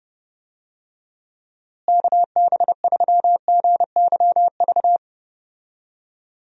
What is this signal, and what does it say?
Morse code, heard as a keyed tone, about 30 wpm, 700 Hz. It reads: KB3GYV
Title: USA Call Signs - 6 Characters - Repeated 2x - 30wpm